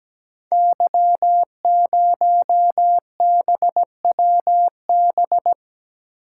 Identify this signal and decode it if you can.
Morse code, heard as a keyed tone, about 17 wpm, 700 Hz. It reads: Y0BWB
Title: Code Groups - 5 Letter:Numbers - 17wpm